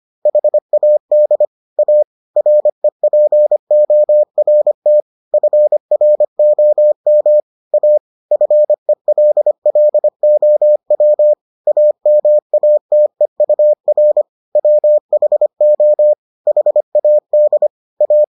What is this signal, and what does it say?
Morse code, heard as a keyed tone, about 25 wpm, 600 Hz. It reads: HAD A REPORT FROM A FELLOW AMATEUR WHO HAD A